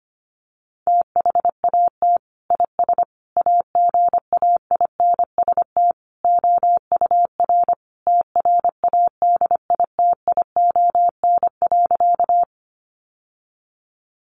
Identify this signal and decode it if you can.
Morse code, heard as a keyed tone, about 25 wpm, 700 Hz. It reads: THAT IS AGAINST OUR TRADITION.
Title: Sentences from Top 1500 Words 25wpm